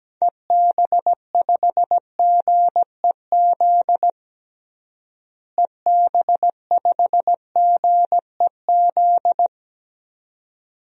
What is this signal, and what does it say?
Morse code, heard as a keyed tone, about 17 wpm, 700 Hz. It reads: EB5GEZ EB5GEZ
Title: DXCC-100 Call Signs - 6 Characters - Repeated 2x - 17wpm